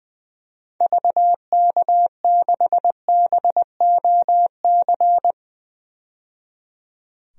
Morse code, heard as a keyed tone, about 21 wpm, 700 Hz. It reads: VK6BOC
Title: DXCC-100 Call Signs - 6 Characters - 20wpm